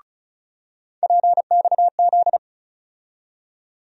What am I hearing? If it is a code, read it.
Morse code, heard as a keyed tone, about 36 wpm, 700 Hz. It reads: PXZ
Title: USA Call Sign Postfixes - 3 Letters 35wpm